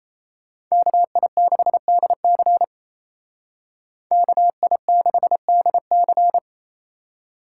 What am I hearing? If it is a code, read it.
Morse code, heard as a keyed tone, about 36 wpm, 700 Hz. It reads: KI6DC KI6DC